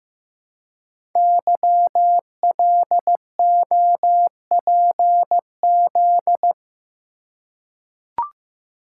Morse code, heard as a keyed tone, about 15 wpm, 700 Hz. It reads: YLOPZ
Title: Code Groups - 5 Letters - 15wpm